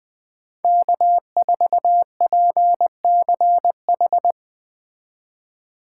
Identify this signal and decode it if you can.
Morse code, heard as a keyed tone, about 20 wpm, 700 Hz. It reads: K4PCH
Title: Speed Racing - USA 1x3 Call Signs 17wpm